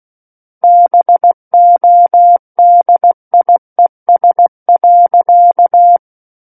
Morse code, heard as a keyed tone, about 16 wpm, 700 Hz. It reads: BODIES.
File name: Book One / 07-Chapter 7 - 16 WPM